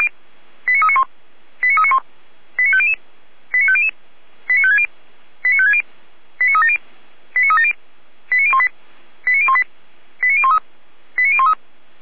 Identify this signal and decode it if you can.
Five-tone selective call (ZVEI), heard as quick runs of five tones, tones of about 70 ms. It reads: 89391 89391 89500 89500 89560 89560 89370 89370 89018 89018 89013 89013